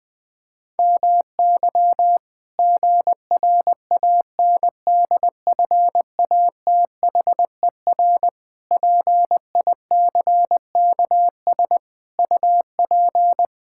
Morse code, heard as a keyed tone, about 20 wpm, 700 Hz. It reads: MY GRANDFATHER PICKS UP
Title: Pangram Challenge v2 - 20wpm